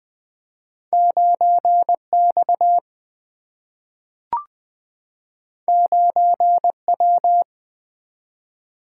Morse code, heard as a keyed tone, about 20 wpm, 700 Hz. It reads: 9X 9W
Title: Single-Letter Single-Number 20wpm